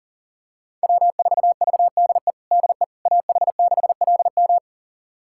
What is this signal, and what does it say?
Morse code, heard as a keyed tone, about 40 wpm, 700 Hz. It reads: W4VDE DE AH6LM
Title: USA Callsign DE Callsign - 5 Characters - 40wpm